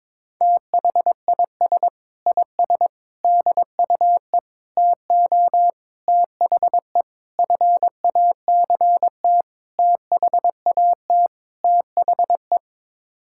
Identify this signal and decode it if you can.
Morse code, heard as a keyed tone, about 22 wpm, 700 Hz. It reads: THIS IS DUE TO THE FACT THAT THE
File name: practice / 22wpm / Sets of 8 Words - 22wpm